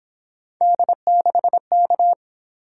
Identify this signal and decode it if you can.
Morse code, heard as a keyed tone, about 28 wpm, 700 Hz. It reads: D6K